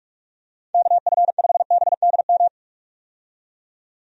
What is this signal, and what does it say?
Morse code, heard as a keyed tone, about 48 wpm, 700 Hz. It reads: KF5BDM
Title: USA Call Signs - 6 Characters - 45wpm